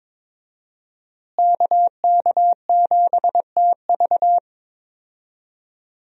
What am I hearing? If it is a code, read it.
Morse code, heard as a keyed tone, about 22 wpm, 700 Hz. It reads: KK7TV
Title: USA Call Signs - 5 Characters - Repeated 3x - 22wpm